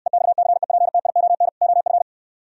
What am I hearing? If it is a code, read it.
Morse code, heard as a keyed tone, about 68 wpm, 700 Hz. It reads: EXCEPTION OF